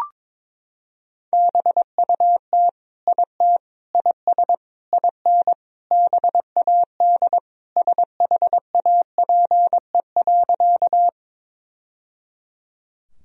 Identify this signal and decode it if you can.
Morse code, heard as a keyed tone, about 22 wpm, 700 Hz. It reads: BUT IT IS IN BAD SHAPE.
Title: Sentences from Top 1000 Words - Review Part 3 - 22wpm